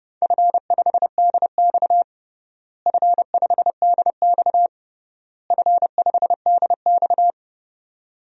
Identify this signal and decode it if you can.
Morse code, heard as a keyed tone, about 30 wpm, 700 Hz. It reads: F5DX F5DX F5DX
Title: DXCC-100 Call Signs - 4 Characters - Repeated 3x - 30wpm